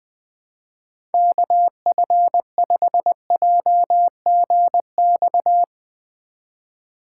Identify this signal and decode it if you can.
Morse code, heard as a keyed tone, about 20 wpm, 700 Hz. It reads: KF5JGX
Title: USA Call Signs - 6 Characters - 4x Character Spacing - 20wpm